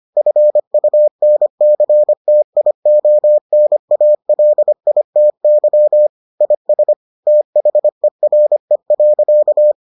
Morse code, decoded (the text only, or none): FUNCTIONALITY IS THERE.